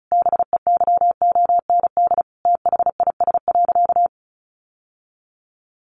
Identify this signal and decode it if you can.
Morse code, heard as a keyed tone, about 34 wpm, 700 Hz. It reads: BEYOND THIS.
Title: Sentences from Top 1200 Words 35wpm